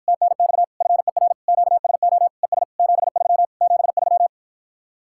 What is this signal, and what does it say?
Morse code, heard as a keyed tone, about 52 wpm, 700 Hz. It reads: TNX FER QSO ES 73 73